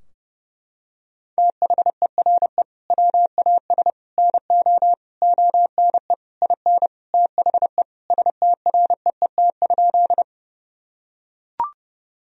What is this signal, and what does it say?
Morse code, heard as a keyed tone, about 31 wpm, 700 Hz. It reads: THERE WAS NO ONE IN THE STREET?
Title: Sentences from Top 1000 Words - Review Part 3 - 30wpm